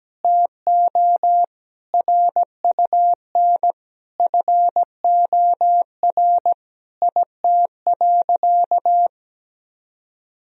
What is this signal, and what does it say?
Morse code, heard as a keyed tone, about 17 wpm, 700 Hz. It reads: TO RUN FOR IT.
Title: Sentences from Top 1000 Words - Review Part 1 - 17wpm